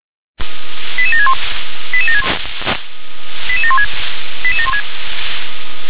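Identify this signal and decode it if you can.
Five-tone selective call (ZVEI), heard as quick runs of five tones, tones of about 70 ms. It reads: 88761 88716 88716
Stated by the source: ZVEI five-tone selective call 88645, received via SDR.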